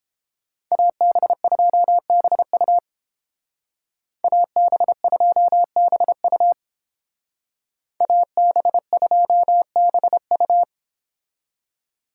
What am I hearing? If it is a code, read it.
Morse code, heard as a keyed tone, about 31 wpm, 700 Hz. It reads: AB2BU AB2BU AB2BU